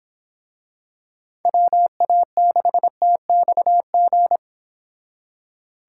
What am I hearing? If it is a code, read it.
Morse code, heard as a keyed tone, about 26 wpm, 700 Hz. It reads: WA6TXG
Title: Speed Racing - USA 2x3 Call Signs 22wpm